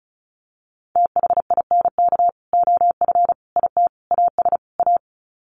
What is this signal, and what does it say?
Morse code, heard as a keyed tone, about 34 wpm, 700 Hz. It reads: THINK OF IT AS A
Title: Sets of 5 Words Encore2 35wpm